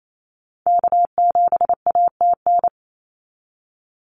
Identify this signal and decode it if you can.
Morse code, heard as a keyed tone, about 28 wpm, 700 Hz. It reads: K7ATN
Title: NA SOTA Activator Callsigns 28wpm